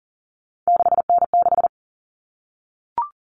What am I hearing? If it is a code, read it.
Morse code, heard as a keyed tone, about 40 wpm, 700 Hz. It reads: 6N6
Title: All Cut Numbers - 3-Digit 40wpm